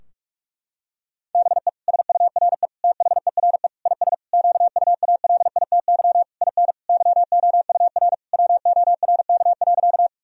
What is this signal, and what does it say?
Morse code, heard as a keyed tone, about 46 wpm, 700 Hz. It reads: BE SURE THERE IS QUALITY IN YOUR WORK.
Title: Sentences from Top 1000 Words - Review Part 4 - 45wpm